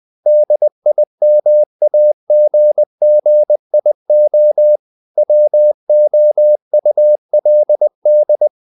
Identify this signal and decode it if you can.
Morse code, heard as a keyed tone, about 20 wpm, 600 Hz. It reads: DIMAGGIO WOULD
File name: the old man and the sea part 2 20wpm